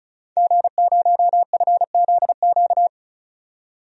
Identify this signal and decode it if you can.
Morse code, heard as a keyed tone, about 36 wpm, 700 Hz. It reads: G0FZQ
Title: DXCC-100 Call Signs - 5 Characters - 35wpm